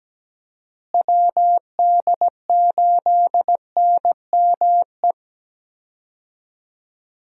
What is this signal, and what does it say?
Morse code, heard as a keyed tone, about 17 wpm, 700 Hz. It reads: WD8NME
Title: USA Call Signs - 6 Characters - Repeated 2x - 17wpm